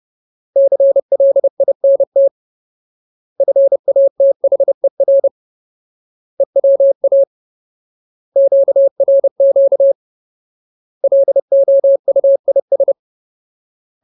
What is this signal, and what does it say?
Morse code, heard as a keyed tone, about 30 wpm, 550 Hz. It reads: CLINT FATHER EWA QRQ LOUIS
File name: Ultimate Word List / 30WPM Ultimate